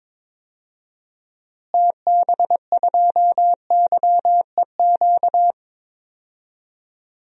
Morse code, heard as a keyed tone, about 22 wpm, 700 Hz. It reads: TB2YEQ
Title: DXCC-100 Call Signs - 6 Characters - Repeated 2x - 22wpm